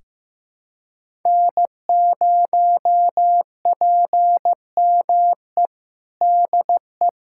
Morse code, heard as a keyed tone, about 15 wpm, 700 Hz. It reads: N0PME DE
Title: USA Callsign DE Callsign - 5 Characters - 15wpm